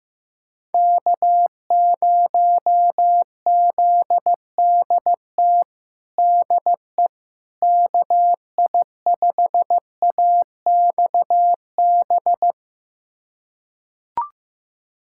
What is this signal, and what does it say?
Morse code, heard as a keyed tone, about 16 wpm, 700 Hz. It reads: K0ZDT DE KI5AXB
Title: USA Callsign DE Callsign 15wpm